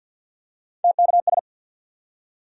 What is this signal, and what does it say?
Morse code, heard as a keyed tone, about 52 wpm, 700 Hz. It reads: TKS
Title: Top 90 QSO Elements 50wpm